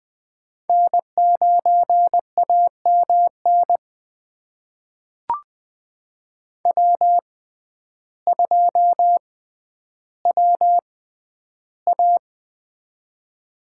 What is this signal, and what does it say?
Morse code, heard as a keyed tone, about 20 wpm, spaced as for 11 wpm, 700 Hz. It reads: N9AMN W2WA
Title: USA Call Signs - 6x Character Spacing - 20wpm